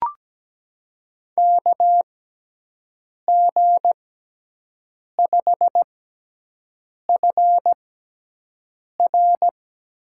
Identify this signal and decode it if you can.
Morse code, heard as a keyed tone, about 17 wpm, spaced as for 6 wpm, 700 Hz. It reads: KG5FR
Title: USA Call Signs - 6 Characters - 6x Character Spacing - 17wpm